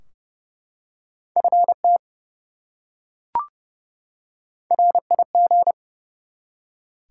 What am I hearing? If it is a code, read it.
Morse code, heard as a keyed tone, about 30 wpm, 700 Hz. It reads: FT RIG
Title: Top 180 QSO Elements 30wpm